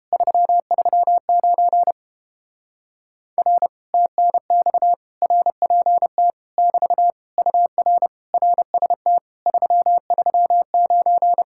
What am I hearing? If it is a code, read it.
Morse code, heard as a keyed tone, about 30 wpm, 700 Hz. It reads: 339 R TNX RPT <BT> UR RST 339